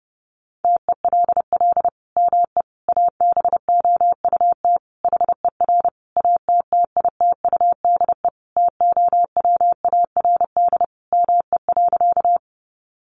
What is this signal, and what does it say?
Morse code, heard as a keyed tone, about 30 wpm, 700 Hz. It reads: TELL ME ABOUT HER ATTITUDE TOWARD ME.
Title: Sentences from Top 1300 Words 30wpm